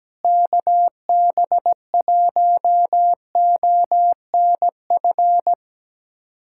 Morse code, heard as a keyed tone, about 17 wpm, 700 Hz. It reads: KB1ONF
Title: USA Callsign DE Callsign - 6 Characters - 17wpm